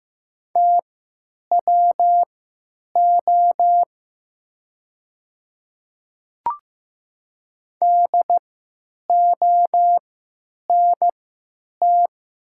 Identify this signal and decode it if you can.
Morse code, heard as a keyed tone, about 15 wpm, spaced as for 9 wpm, 700 Hz. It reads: TWO DONT
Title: 3x Character Spacing - Top 300 Words - 15wpm